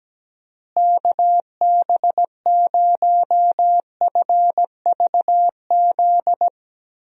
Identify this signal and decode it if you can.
Morse code, heard as a keyed tone, about 17 wpm, 700 Hz. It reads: KB0FVZ
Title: USA Call Signs - 6 Characters - Repeated 3x - 17wpm